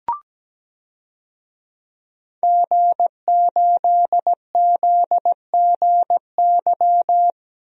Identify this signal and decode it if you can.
Morse code, heard as a keyed tone, about 17 wpm, 700 Hz. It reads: G8ZGY
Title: DXCC-100 Call Signs - 5 Characters - 17wpm